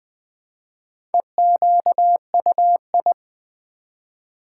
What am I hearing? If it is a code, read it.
Morse code, heard as a keyed tone, about 20 wpm, 700 Hz. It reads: EQUI
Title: Build A Word - 1 Letter At A Time - 5-Letter Words - 20wpm